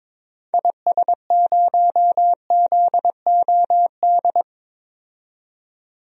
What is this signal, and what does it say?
Morse code, heard as a keyed tone, about 22 wpm, 700 Hz. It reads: IS0ZOD
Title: DXCC-100 Call Signs - 6 Characters - 22wpm